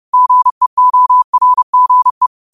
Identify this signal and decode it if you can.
Morse code, heard as a keyed tone, about 30 wpm, 1000 Hz. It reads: GEORGE